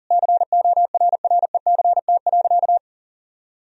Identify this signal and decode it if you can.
Morse code, heard as a keyed tone, about 40 wpm, 700 Hz. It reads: CORRECT.
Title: Sentences from Top 1000 Words - Review Part 2 - 40wpm